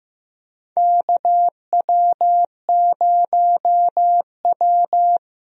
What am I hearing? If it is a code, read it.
Morse code, heard as a keyed tone, about 15 wpm, 700 Hz. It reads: KW0W
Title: USA Call Signs - 4 Characters - Repeated 3x - 15wpm